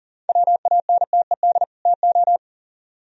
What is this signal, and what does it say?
Morse code, heard as a keyed tone, about 52 wpm, spaced as for 46 wpm, 700 Hz. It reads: WANTED TO
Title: Speed-Racing - Sets of 2 Words 40wpm